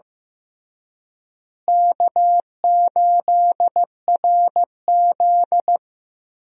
Morse code, heard as a keyed tone, about 16 wpm, 700 Hz. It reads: K8RZ
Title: USA Call Signs - 4 Characters - Repeated 3x - 15wpm